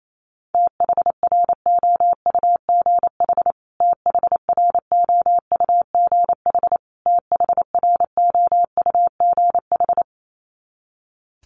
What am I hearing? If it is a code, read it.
Morse code, heard as a keyed tone, about 28 wpm, 700 Hz. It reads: THROUGH THROUGH THROUGH
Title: Top 300 Words Repeated 28wpm